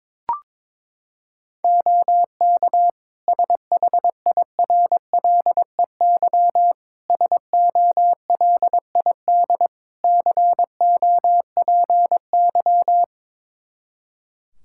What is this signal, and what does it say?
Morse code, heard as a keyed tone, about 22 wpm, 700 Hz. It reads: OK SHIRLEY SOLID COPY
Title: Practice QSO - Exchange 3 - More Info - 22wpm